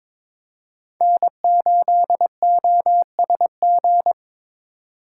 Morse code, heard as a keyed tone, about 26 wpm, 700 Hz. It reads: N8OSG
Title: Speed Racing - USA Call Signs 22wpm